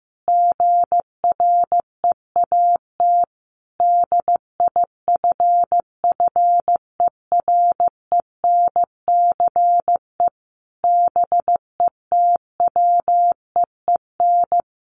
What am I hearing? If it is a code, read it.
Morse code, heard as a keyed tone, about 15 wpm, 700 Hz. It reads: GREAT DIFFERENCE BETWEEN
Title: Sentences from Top 800 Words 15wpm